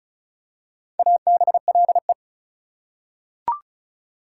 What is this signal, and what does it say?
Morse code, heard as a keyed tone, about 34 wpm, 700 Hz. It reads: ABLE
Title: Prefixes and Suffixes 35wpm